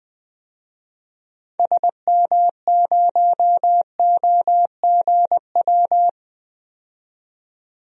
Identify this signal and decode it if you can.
Morse code, heard as a keyed tone, about 20 wpm, 700 Hz. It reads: SM0OGW
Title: DXCC-100 Call Signs - 6 Characters - 20wpm